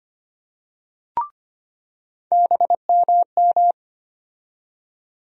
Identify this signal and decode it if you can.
Morse code, heard as a keyed tone, about 25 wpm, 700 Hz. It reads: BMM
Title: USA Call Sign Postfixes - 3 Letters 25wpm